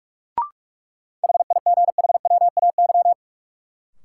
Morse code, heard as a keyed tone, about 46 wpm, 700 Hz. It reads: HIGHWAY
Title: Sentences from Top 1400 Words - New Word Drill - 45wpm